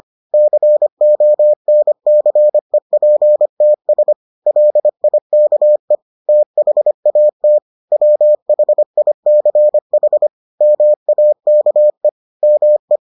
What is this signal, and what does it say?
Morse code, heard as a keyed tone, about 25 wpm, 600 Hz. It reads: CONCEPTS LIKE THAT WHICH MAKE ME